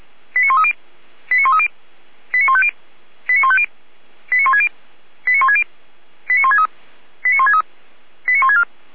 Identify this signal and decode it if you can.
Five-tone selective call (ZVEI), heard as quick runs of five tones, tones of about 70 ms. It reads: 89130 89130 89150 89150 89160 89160 89163 89163 89164